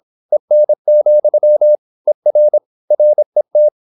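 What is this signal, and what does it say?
Morse code, heard as a keyed tone, about 26 wpm, 600 Hz. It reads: EN, ER RET
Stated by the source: High-speed (QRQ) Morse at 26 wpm.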